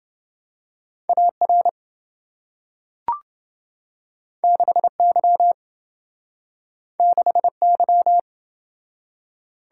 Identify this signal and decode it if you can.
Morse code, heard as a keyed tone, about 27 wpm, 700 Hz. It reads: AR 6Y 6Y